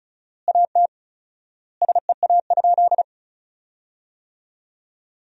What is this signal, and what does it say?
Morse code, heard as a keyed tone, about 34 wpm, 700 Hz. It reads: AT SEA?
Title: 4x Word-Spacing - Sentences from Top 300 Words - 35wpm